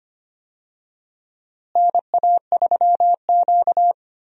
Morse code, heard as a keyed tone, about 25 wpm, 700 Hz. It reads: NA3Q
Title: USA Call Signs - 4 Characters - Repeated 2x - 25wpm